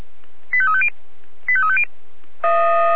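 Five-tone selective call (ZVEI), heard as quick runs of five tones, tones of about 70 ms. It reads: 85350 85350
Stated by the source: ZVEI five-tone call 85888 received on an SDR, channel 496.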